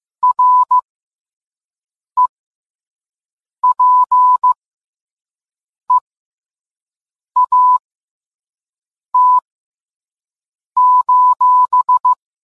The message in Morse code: REPEAT: